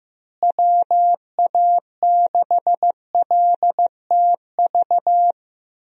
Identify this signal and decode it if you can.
Morse code, heard as a keyed tone, about 14 wpm, 700 Hz. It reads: WA6LTV
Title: USA Call Signs - 6 Characters - Repeated 3x - 15wpm